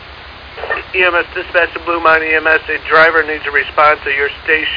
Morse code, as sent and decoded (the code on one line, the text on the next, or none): none